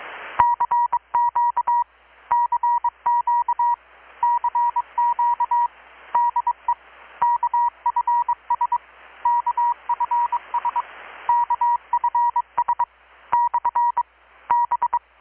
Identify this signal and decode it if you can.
Morse code, heard as a keyed tone, about 23 wpm, 960 Hz. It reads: CQ CQ CQ DE KFS KFS KFS / B
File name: blog / swl / 6-abril-2019-12695-USB-KPS